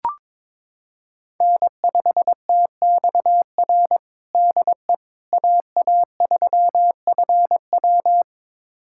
Morse code, heard as a keyed tone, about 21 wpm, 700 Hz. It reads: N5TXR DE AA3FW
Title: USA Callsign DE Callsign - 5 Characters - 22wpm